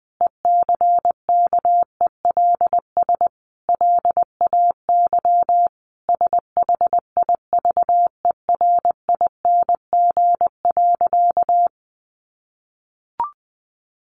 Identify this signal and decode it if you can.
Morse code, heard as a keyed tone, about 20 wpm, 700 Hz. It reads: ECKELS LAY SHIVERING.
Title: A Sound Of Thunder 20wpm